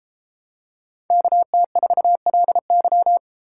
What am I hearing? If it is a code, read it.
Morse code, heard as a keyed tone, about 31 wpm, 700 Hz. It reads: KT4LY